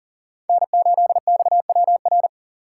Morse code, heard as a keyed tone, about 40 wpm, 700 Hz. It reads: N8XWR
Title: USA Call Signs - 4x Character Spacing - 40wpm